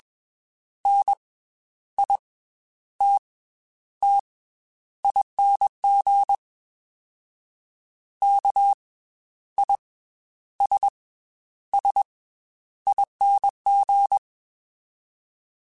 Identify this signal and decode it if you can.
Morse code, heard as a keyed tone, about 21 wpm, 800 Hz. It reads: N I T T ING K I S S ING
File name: Special INGs / Special INGs-mp3s / Slow ing @20